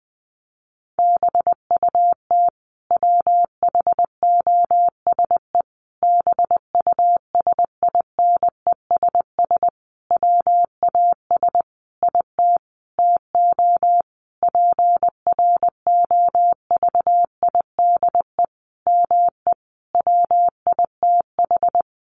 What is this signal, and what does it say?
Morse code, heard as a keyed tone, about 20 wpm, 700 Hz. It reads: BUT WHOSE BUSINESS WAS IT TO PROVIDE ME WITH
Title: Sentences from Top 1500 Words 20wpm